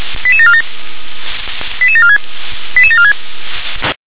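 Five-tone selective call (ZVEI), heard as quick runs of five tones, tones of about 70 ms. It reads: 88747 88746 88746
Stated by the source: Five-tone ZVEI selective call 88603, received by SDR.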